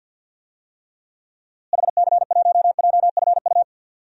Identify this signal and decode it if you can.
Morse code, heard as a keyed tone, about 51 wpm, 700 Hz. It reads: HC1JFU
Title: DXCC-100 Call Signs - 6 Characters - Repeated 3x - 50wpm